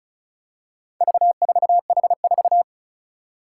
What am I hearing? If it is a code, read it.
Morse code, heard as a keyed tone, about 36 wpm, 700 Hz. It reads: V4H4
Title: Four Challenging Letter-Number Combinations 35wpm